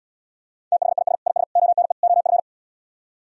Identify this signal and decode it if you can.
Morse code, heard as a keyed tone, about 78 wpm, 690 Hz. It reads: THIS IS ONE OF